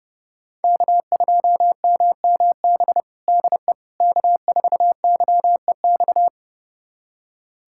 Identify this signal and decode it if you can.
Morse code, heard as a keyed tone, about 30 wpm, 700 Hz. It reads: K2MMB DE K4YEX
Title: USA Callsign DE Callsign - 5 Characters - 30wpm